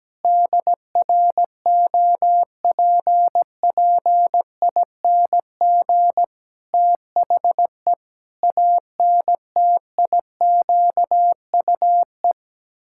Morse code, heard as a keyed tone, about 17 wpm, 700 Hz. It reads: DROPPING THE ANTIQUE
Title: Pangram Challenge v2 - 17wpm